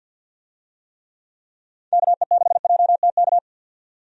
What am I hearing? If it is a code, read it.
Morse code, heard as a keyed tone, about 50 wpm, 700 Hz. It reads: KE6JTK
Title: USA Call Signs - 6 Characters - 50wpm